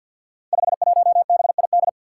Morse code, heard as a keyed tone, about 49 wpm, 700 Hz. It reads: 51BID
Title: Five Challenging Letter-Number Combinations 50wpm